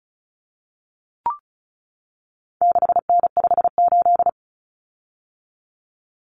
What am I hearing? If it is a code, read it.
Morse code, heard as a keyed tone, about 34 wpm, 700 Hz. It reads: 6N58
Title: Common Cut Numbers - 4-Digit 35wpm